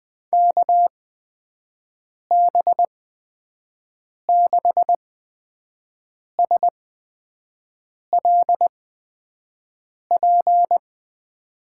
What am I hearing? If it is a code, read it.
Morse code, heard as a keyed tone, about 20 wpm, spaced as for 5 wpm, 700 Hz. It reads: KB6SLP
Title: USA Call Signs - 6 Characters - 8x Character Spacing - 20wpm